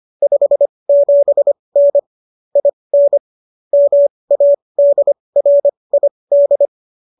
Morse code, heard as a keyed tone, about 25 wpm, spaced as for 20 wpm, 570 Hz. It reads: 57N IN MADRID